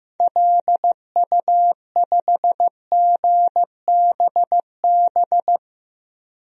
Morse code, heard as a keyed tone, about 15 wpm, 700 Hz. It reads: LU5GBB
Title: DXCC-100 Call Signs - 6 Characters - 15wpm